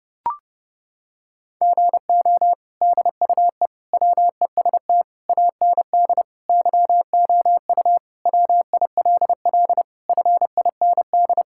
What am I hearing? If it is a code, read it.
Morse code, heard as a keyed tone, about 30 wpm, 700 Hz. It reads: GO DUE WEST AND YOU WILL FIND
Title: Sentences from Top 1000 Words - Review Part 6 - 30wpm